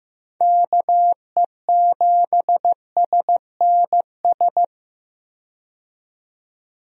Morse code, heard as a keyed tone, about 15 wpm, 700 Hz. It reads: KE7SNS
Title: USA Call Signs - 6 Characters - 15wpm